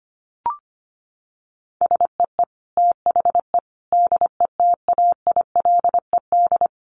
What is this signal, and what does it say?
Morse code, heard as a keyed tone, about 25 wpm, 700 Hz. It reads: SEE THE DETAILED
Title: Sentences from Top 1500 Words 25wpm